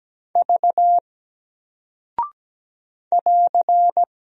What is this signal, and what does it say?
Morse code, heard as a keyed tone, about 17 wpm, 700 Hz. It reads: V <AR>
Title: Single Character (Letter:Number:Punctuation:Prosign) 17wpm